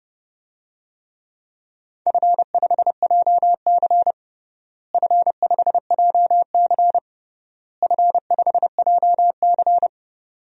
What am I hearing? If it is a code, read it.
Morse code, heard as a keyed tone, about 32 wpm, 700 Hz. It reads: F5JC F5JC F5JC
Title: DXCC-100 Call Signs - 4 Characters - Repeated 3x - 30wpm